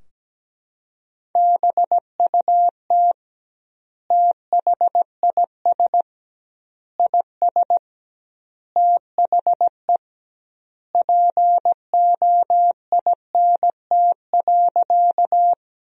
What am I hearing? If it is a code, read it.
Morse code, heard as a keyed tone, about 17 wpm, 700 Hz. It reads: BUT THIS IS THE POINT.
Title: 2x Word-Spacing - Sentences from Top 200 Words - 17wpm